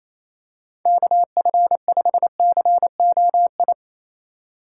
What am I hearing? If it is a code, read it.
Morse code, heard as a keyed tone, about 28 wpm, 700 Hz. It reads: KF5COI
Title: USA Call Signs - 6 Characters - 2x Character Spacing - 28wpm